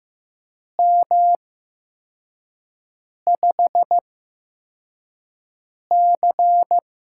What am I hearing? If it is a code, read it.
Morse code, heard as a keyed tone, about 15 wpm, spaced as for 4 wpm, 700 Hz. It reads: M5C
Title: USA Call Signs - 4 Characters - 8x Character Spacing - 15wpm